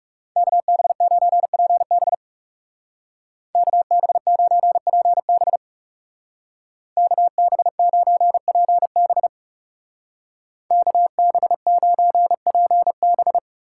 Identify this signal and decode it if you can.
Morse code, heard as a keyed tone, about 35 wpm, 700 Hz. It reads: KB9PB KB9PB KB9PB KB9PB
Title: Speed Racing - USA 2x2 Call Signs 30wpm